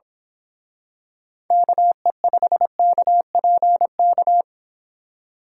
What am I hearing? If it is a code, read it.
Morse code, heard as a keyed tone, about 26 wpm, 700 Hz. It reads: KE5KPK
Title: Speed Racing - USA Call Signs 22wpm